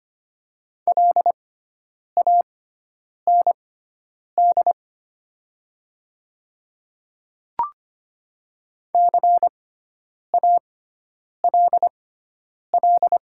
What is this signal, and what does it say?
Morse code, heard as a keyed tone, about 25 wpm, spaced as for 9 wpm, 700 Hz. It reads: LAND CALL
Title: 6x Character Spacing - Top 500 Words - 25wpm